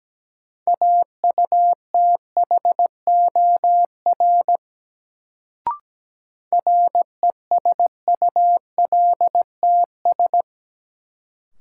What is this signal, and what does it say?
Morse code, heard as a keyed tone, about 17 wpm, 700 Hz. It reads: AUTHOR RESULTS
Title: Sets of 4 Words - New Word Drill - 17wpm